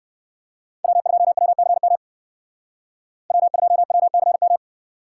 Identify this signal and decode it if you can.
Morse code, heard as a keyed tone, about 59 wpm, 700 Hz. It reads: W2WCM W2WCM